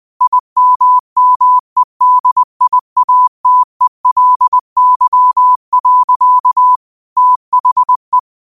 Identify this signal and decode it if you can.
Morse code, heard as a keyed tone, about 20 wpm, 1000 Hz. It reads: IMMEDIATELY. THE